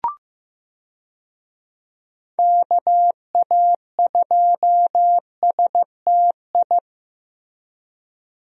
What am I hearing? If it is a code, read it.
Morse code, heard as a keyed tone, about 15 wpm, 700 Hz. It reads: KA2STI
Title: USA Call Signs - 6 Characters - Repeated 2x - 15wpm